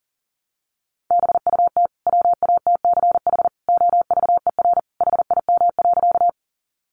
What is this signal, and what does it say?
Morse code, heard as a keyed tone, about 40 wpm, 700 Hz. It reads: BUT WATCH OVER HIM.
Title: Sentences from Top 300 Words 40wpm